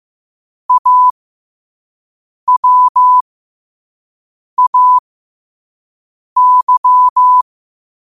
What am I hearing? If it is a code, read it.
Morse code, heard as a keyed tone, about 15 wpm, spaced as for 5 wpm, 1000 Hz. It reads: AWAY
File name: qotdfiles / 5wpm